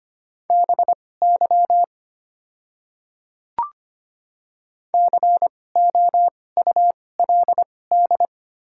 Morse code, heard as a keyed tone, about 25 wpm, spaced as for 18 wpm, 700 Hz. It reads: BY COULD
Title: 2x Character Spacing - Top 200 Words - 25wpm